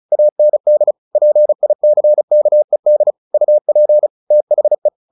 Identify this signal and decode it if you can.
Morse code, heard as a keyed tone, about 34 wpm, 600 Hz. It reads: AND PICKED UP THE
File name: the old man and the sea 2 35wpm